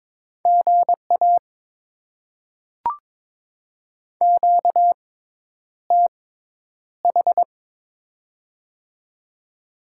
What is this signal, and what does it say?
Morse code, heard as a keyed tone, about 22 wpm, spaced as for 10 wpm, 700 Hz. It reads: GA QTH
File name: Top 20 QSO Elements - 6x Character Spacing - 22wpm